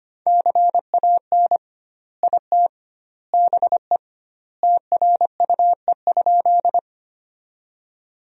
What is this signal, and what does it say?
Morse code, heard as a keyed tone, about 25 wpm, 700 Hz. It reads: CAN IT BE TRUE?
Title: 2x Word-Spacing - Sentences from Top 400 Words - 25wpm